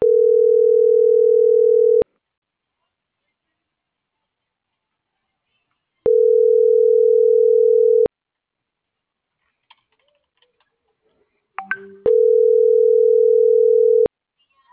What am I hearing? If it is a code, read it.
Ringback tone (North America), 440 + 480 Hz.
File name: CENTRAL SAGOT / out miriam.mercado